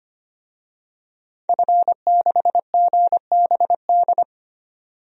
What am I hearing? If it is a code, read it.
Morse code, heard as a keyed tone, about 25 wpm, 700 Hz. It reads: F6GBD
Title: DXCC-100 Call Signs - 5 Characters - 25wpm